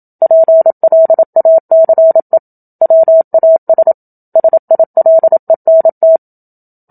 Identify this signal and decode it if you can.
Morse code, heard as a keyed tone, about 27 wpm, 650 Hz. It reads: PLACE WAS SILENT
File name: Book Two / 01-Chapter 1 - 27 WPM